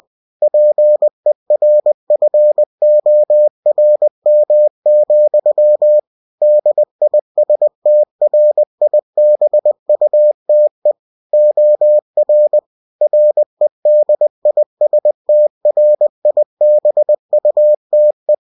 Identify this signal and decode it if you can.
Morse code, heard as a keyed tone, about 20 wpm, 600 Hz. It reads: PERFORM, DISTRIBUTE OR REDISTRIBUTE